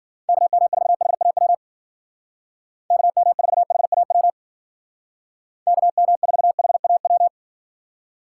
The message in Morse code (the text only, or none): KM4HAW KM4HAW KM4HAW